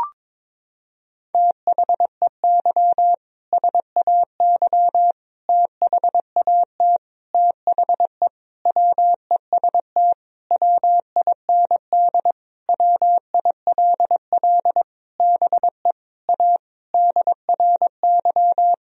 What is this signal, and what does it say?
Morse code, heard as a keyed tone, about 22 wpm, 700 Hz. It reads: THEY SAY THAT THE WEST WIND WILL BE A DRY
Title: Sentences from Top 1000 Words - Review Part 3 - 22wpm